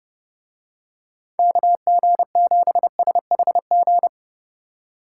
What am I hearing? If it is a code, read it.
Morse code, heard as a keyed tone, about 29 wpm, 700 Hz. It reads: KG7SHG